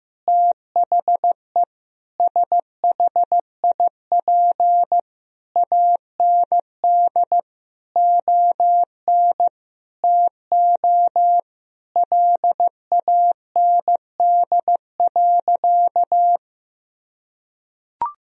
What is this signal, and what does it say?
Morse code, heard as a keyed tone, about 16 wpm, 700 Hz. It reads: THE SHIP AND ON TO LAND.
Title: Sentences from Top 1000 Words - Review Part 2 - 15wpm